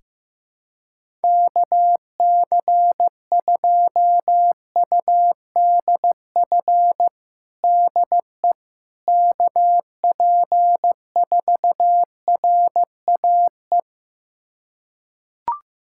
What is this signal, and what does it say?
Morse code, heard as a keyed tone, about 15 wpm, 700 Hz. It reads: KC2UDF DE KP4RAE
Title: USA Callsign DE Callsign - 6 Characters - 15wpm